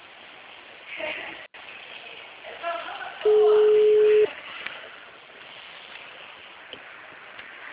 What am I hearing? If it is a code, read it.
Ringback tone (Europe, 425 Hz).